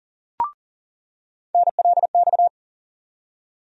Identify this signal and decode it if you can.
Morse code, heard as a keyed tone, about 40 wpm, 700 Hz. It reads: NLX
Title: USA Call Sign Postfixes - 3 Letters 40wpm